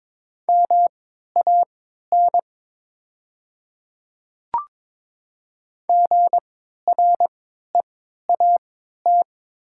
Morse code, heard as a keyed tone, about 22 wpm, spaced as for 13 wpm, 700 Hz. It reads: MAN GREAT
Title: 3x Character Spacing - Top 200 Words - 22wpm